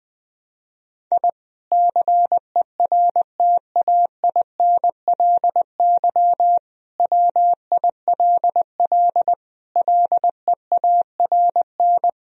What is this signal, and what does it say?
Morse code, heard as a keyed tone, about 20 wpm, 700 Hz. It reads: I CERTAINLY WILL LEARN
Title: Sentences from Top 1000 Words - Review Part 4 - 20wpm